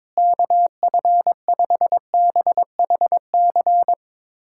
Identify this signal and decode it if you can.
Morse code, heard as a keyed tone, about 22 wpm, 700 Hz. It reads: KF5BHC
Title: USA Callsign DE Callsign 22wpm